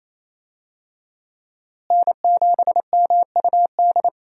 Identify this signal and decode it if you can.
Morse code, heard as a keyed tone, about 28 wpm, 700 Hz. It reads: N7MUD
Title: USA Call Signs - 5 Characters - Repeated 2x - 28wpm